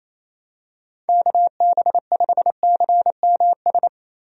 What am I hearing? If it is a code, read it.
Morse code, heard as a keyed tone, about 28 wpm, 700 Hz. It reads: KB5CMS
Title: USA Call Signs - 6 Characters - Repeated 3x - 28wpm